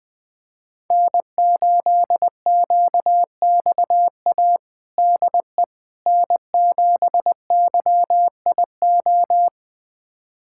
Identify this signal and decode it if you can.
Morse code, heard as a keyed tone, about 21 wpm, 700 Hz. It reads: N8QXA DE N7YIO
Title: USA Callsign DE Callsign - 5 Characters - 20wpm